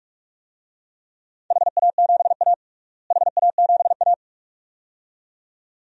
Morse code, heard as a keyed tone, about 46 wpm, 700 Hz. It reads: HA7A HA7A
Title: DXCC-100 Call Signs - 4 Characters - Repeated 2x - 45wpm